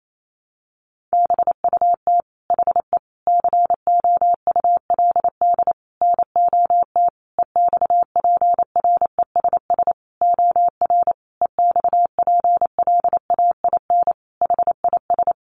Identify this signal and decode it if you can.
Morse code, heard as a keyed tone, about 28 wpm, 700 Hz. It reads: BUT HE COULD NOT EXPRESS OR EXPLAIN HIS
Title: Sentences from Top 1100 Words 28wpm